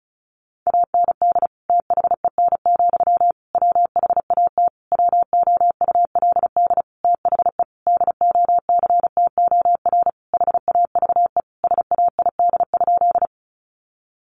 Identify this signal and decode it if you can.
Morse code, heard as a keyed tone, about 34 wpm, 700 Hz. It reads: AND THEN, WHAT WOULD THE DOCTOR HAVE SAID?
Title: Sentences from Top 1500 Words 35wpm